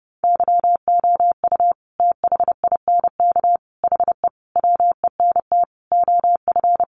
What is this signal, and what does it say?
Morse code, heard as a keyed tone, about 30 wpm, 700 Hz. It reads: YOU THINK HE WENT OF
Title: Sentences from Top 800 Words 30wpm